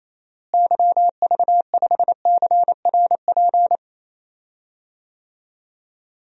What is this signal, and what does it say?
Morse code, heard as a keyed tone, about 28 wpm, 700 Hz. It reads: YV5CRP
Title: DXCC-100 Call Signs - 6 Characters - Repeated 3x - 28wpm